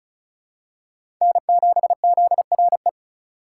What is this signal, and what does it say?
Morse code, heard as a keyed tone, about 34 wpm, 700 Hz. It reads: N7ZRE
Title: USA Call Signs - 5 Characters - 3x Character Spacing - 35wpm